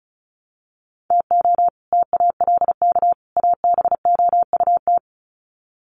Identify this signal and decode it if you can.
Morse code, heard as a keyed tone, about 34 wpm, 700 Hz. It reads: TO TALK ABOUT
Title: Sets of 3 Words 35wpm